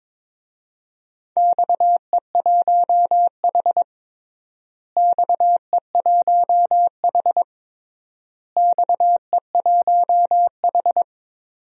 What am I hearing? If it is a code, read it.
Morse code, heard as a keyed tone, about 22 wpm, 700 Hz. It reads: XE1H XE1H XE1H
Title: DXCC-100 Call Signs - 4 Characters - Repeated 3x - 22wpm